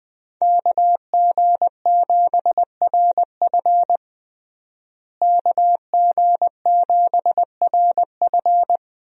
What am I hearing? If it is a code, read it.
Morse code, heard as a keyed tone, about 21 wpm, 700 Hz. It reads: KG7RF KG7RF
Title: USA Call Signs - 5 Characters - Repeated 3x - 20wpm